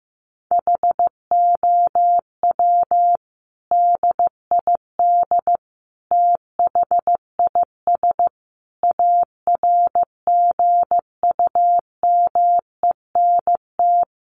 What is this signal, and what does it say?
Morse code, heard as a keyed tone, about 15 wpm, 700 Hz. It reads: HOW DID THIS ARGUMENT
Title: Sentences from Top 1400 Words 15wpm